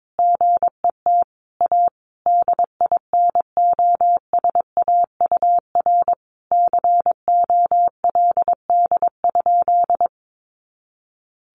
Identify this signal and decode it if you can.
Morse code, heard as a keyed tone, about 22 wpm, 700 Hz. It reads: GET A DINOSAUR COLD?
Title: A Sound Of Thunder 22wpm